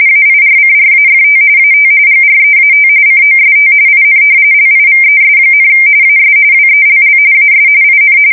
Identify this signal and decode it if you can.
RTTY, about 45.45 baud. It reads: AIRCRAFT AT GREAT COST IN LIVES, AS WELL AS IN
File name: fsk170 / battleshipnj